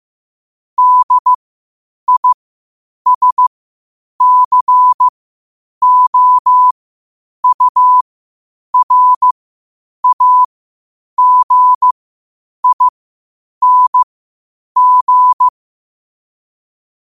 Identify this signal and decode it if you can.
Morse code, heard as a keyed tone, about 15 wpm, spaced as for 8 wpm, 1000 Hz. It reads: DISCOURAGING
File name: qotdfiles / 8wpm